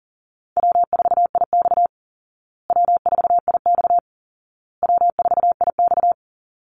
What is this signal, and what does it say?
Morse code, heard as a keyed tone, about 40 wpm, 700 Hz. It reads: W4IX W4IX W4IX
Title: Call Signs Repeated 2x - 40wpm